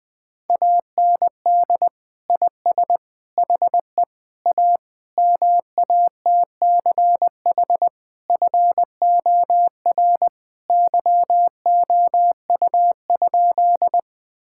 Morse code, heard as a keyed tone, about 20 wpm, 700 Hz. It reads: AND IS HE A MATCH FOR YOU?
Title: Sentences from Top 1600 Words 20wpm